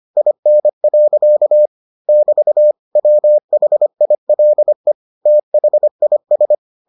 Morse code, heard as a keyed tone, about 25 wpm, 600 Hz. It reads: IN. <BT> WHILE THIS